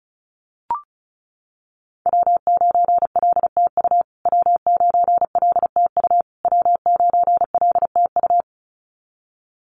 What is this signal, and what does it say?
Morse code, heard as a keyed tone, about 34 wpm, 700 Hz. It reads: W9LTU W9LTU W9LTU
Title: USA Call Signs Repeated 3x 35wpm